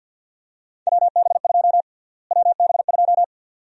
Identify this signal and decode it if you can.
Morse code, heard as a keyed tone, about 51 wpm, 700 Hz. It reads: WB2 WB2
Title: USA Call Sign Prefixes - Repeated 2x - 50wpm